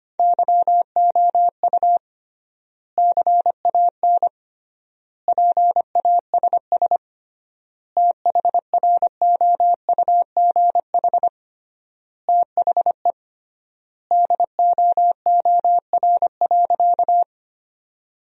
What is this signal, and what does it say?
Morse code, heard as a keyed tone, about 25 wpm, 700 Hz. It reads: YOU CAN PASS THROUGH THE DOOR.
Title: 3x Word-Spacing - Sentences from Top 400 Words - 25wpm